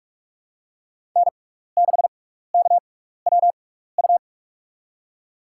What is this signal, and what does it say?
Morse code, heard as a keyed tone, about 46 wpm, spaced as for 16 wpm, 700 Hz. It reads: N6KWU
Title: USA Call Signs - 5 Characters - 6x Character Spacing - 45wpm